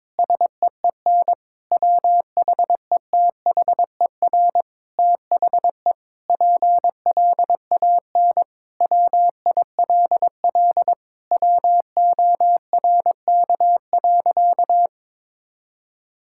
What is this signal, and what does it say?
Morse code, heard as a keyed tone, about 22 wpm, 700 Hz. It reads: SEEN WHETHER THE PLAN WILL WORK.
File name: Sentences with 7-Word Phrases - 22wpm